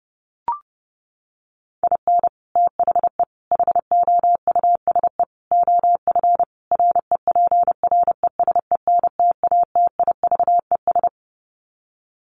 Morse code, heard as a keyed tone, about 28 wpm, 700 Hz. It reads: IN THE HOUSE OF REPRESENTATIVES
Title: Sets of 5 Words Encore 30wpm